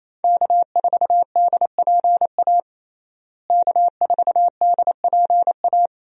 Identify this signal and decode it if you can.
Morse code, heard as a keyed tone, about 28 wpm, 700 Hz. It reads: K4DPA K4DPA
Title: USA Call Signs - 5 Characters - Repeated 2x - 28wpm